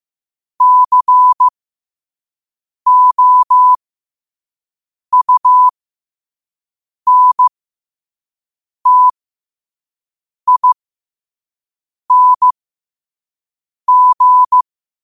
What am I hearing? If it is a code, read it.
Morse code, heard as a keyed tone, about 15 wpm, spaced as for 5 wpm, 1000 Hz. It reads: COUNTING